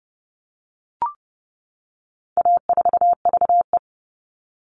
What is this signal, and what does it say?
Morse code, heard as a keyed tone, about 30 wpm, 700 Hz. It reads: A4VE
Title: All Cut Numbers - 4-Digit 30wpm